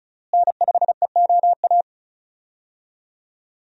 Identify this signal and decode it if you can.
Morse code, heard as a keyed tone, about 36 wpm, 700 Hz. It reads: N5EOA
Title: USA Call Signs - 5 Characters - Repeated 2x - 35wpm